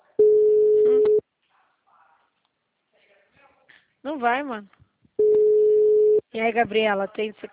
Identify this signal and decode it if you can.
Ringback tone (Europe, 425 Hz), with speech.
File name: out- -2005- - - .174